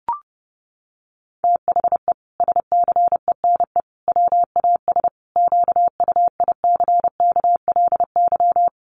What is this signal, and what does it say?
Morse code, heard as a keyed tone, about 30 wpm, 700 Hz. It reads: THE SCENE WAS QUICKLY
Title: Sentences from Top 1100 Words 30wpm